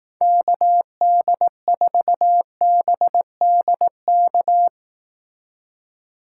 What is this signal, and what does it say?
Morse code, heard as a keyed tone, about 18 wpm, 700 Hz. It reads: KD4BDK
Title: Speed Racing - USA 2x3 Call Signs 15wpm